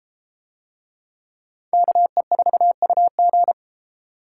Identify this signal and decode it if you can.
Morse code, heard as a keyed tone, about 31 wpm, 700 Hz. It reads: KE4UG